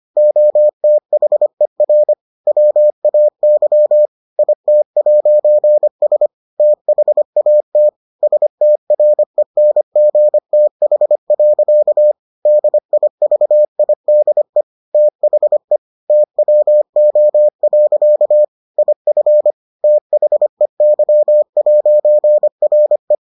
Morse code, heard as a keyed tone, about 25 wpm, 600 Hz. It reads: OTHER WAY IT'S THAT STRENGTH. DIVIDE THE TWO. IF THEY'RE